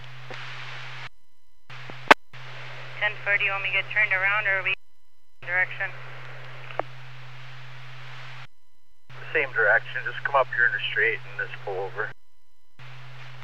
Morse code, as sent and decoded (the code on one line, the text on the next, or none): none